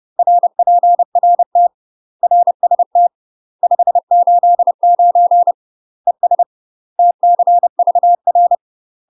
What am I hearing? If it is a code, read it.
Morse code, heard as a keyed tone, about 30 wpm, 700 Hz. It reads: RPRT RST 589 ES TCVR
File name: UK MASTER QSOs TRAINING / 30WPM / UK MASTER QSOs 0019